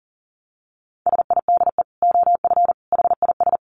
Morse code, heard as a keyed tone, about 40 wpm, 700 Hz. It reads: SIDE OF HIS
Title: Dit Challenge 40wpm